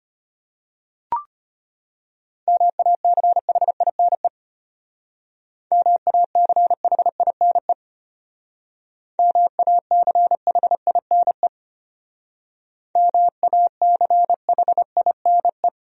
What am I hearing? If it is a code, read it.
Morse code, heard as a keyed tone, about 33 wpm, 700 Hz. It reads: MACHINE MACHINE MACHINE MACHINE